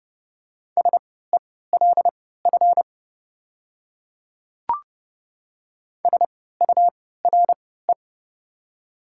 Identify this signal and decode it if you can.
Morse code, heard as a keyed tone, about 29 wpm, spaced as for 17 wpm, 700 Hz. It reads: SELF SURE
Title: 3x Character Spacing - Top 500 Words - 30wpm